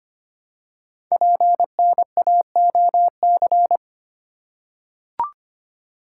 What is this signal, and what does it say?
Morse code, heard as a keyed tone, about 25 wpm, 700 Hz. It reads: PNAOC
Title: Code Groups - 5 Letters - 25wpm